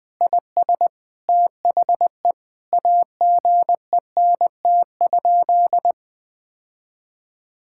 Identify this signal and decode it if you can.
Morse code, heard as a keyed tone, about 20 wpm, 700 Hz. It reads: IS THE AGENT?
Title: Sentences from Top 1600 Words 20wpm